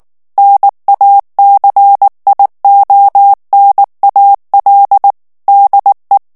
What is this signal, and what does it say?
Morse code, heard as a keyed tone, about 19 wpm, 800 Hz. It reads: NACIONAL DE
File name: morse mp3 / ESPAGNOL TEXTE EN CLAIR 2 VITESSE 18 MOTS MINUTES